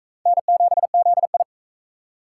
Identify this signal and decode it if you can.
Morse code, heard as a keyed tone, about 39 wpm, 700 Hz. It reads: N7ZI